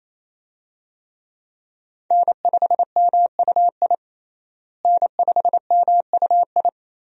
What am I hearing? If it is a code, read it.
Morse code, heard as a keyed tone, about 28 wpm, 700 Hz. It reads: N5MUI N5MUI
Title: USA Call Signs - 5 Characters - Repeated 2x - 28wpm